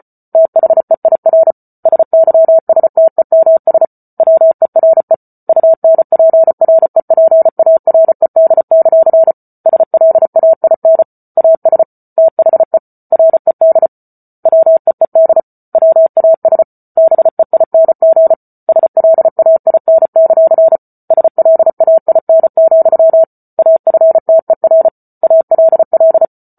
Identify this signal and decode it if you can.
Morse code, heard as a keyed tone, about 34 wpm, 650 Hz. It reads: THEIR SYSTEMS WERE UNPREPARED; SLAIN AS THE RED WEED WAS BEING SLAIN; SLAIN, AFTER ALL